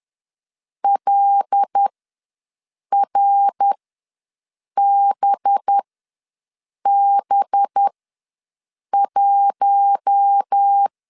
Morse code, heard as a keyed tone, about 10 wpm, spaced as for 6 wpm, 800 Hz. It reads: LRBB1
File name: LEZIONI / 5 LBVC / 5 LB ws 10wpm 3616